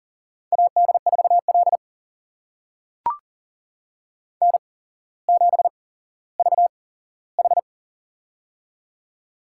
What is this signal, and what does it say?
Morse code, heard as a keyed tone, about 42 wpm, spaced as for 17 wpm, 700 Hz. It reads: AD4L N7VH
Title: USA Call Signs - 4 Characters - 8x Character Spacing - 40wpm